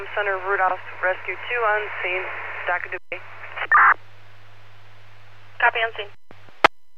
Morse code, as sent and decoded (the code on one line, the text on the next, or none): none